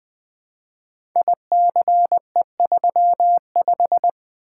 Morse code, heard as a keyed tone, about 20 wpm, 700 Hz. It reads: ICE35
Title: Code Groups - 5 Letter:Numbers - 20wpm